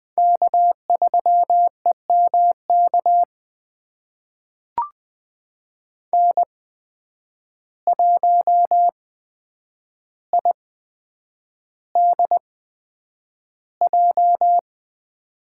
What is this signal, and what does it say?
Morse code, heard as a keyed tone, about 20 wpm, spaced as for 9 wpm, 700 Hz. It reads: K3EMK N1IDJ
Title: USA Call Signs - 5 Characters - 8x Character Spacing - 20wpm